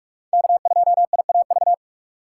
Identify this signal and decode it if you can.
Morse code, heard as a keyed tone, about 43 wpm, 700 Hz. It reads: K2IAV
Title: USA Call Signs - 5 Characters - 2x Character Spacing - 45wpm